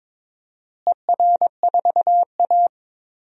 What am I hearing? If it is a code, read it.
Morse code, heard as a keyed tone, about 22 wpm, 700 Hz. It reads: ER4A
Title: DXCC-100 Call Signs - 4 Characters - Repeated 2x - 22wpm